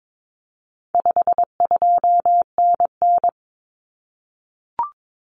Morse code, heard as a keyed tone, about 22 wpm, 700 Hz. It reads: 52NN
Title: Common Cut Numbers - 4-Digit 22wpm